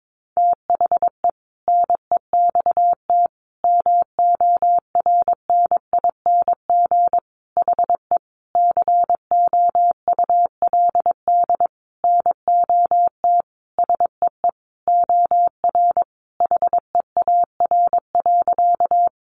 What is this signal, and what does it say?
Morse code, heard as a keyed tone, about 22 wpm, 700 Hz. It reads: THE NEXT MORNING HE COULD NOT SEE OR HEAR.
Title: Sentences from Top 600 Words 22wpm